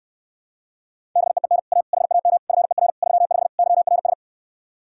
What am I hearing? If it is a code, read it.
Morse code, heard as a keyed tone, about 69 wpm, 700 Hz. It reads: BEEN A HAM FER 35 YRS